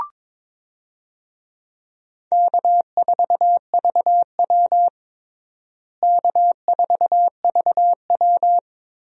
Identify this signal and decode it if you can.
Morse code, heard as a keyed tone, about 22 wpm, 700 Hz. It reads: K4VW K4VW
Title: USA Call Signs - 4 Characters - Repeated 2x - 22wpm